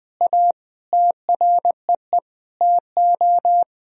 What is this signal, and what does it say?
Morse code, heard as a keyed tone, about 21 wpm, 700 Hz. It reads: A TREE TO
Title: Sentences from Top 1000 Words - Review Part 1 - 20wpm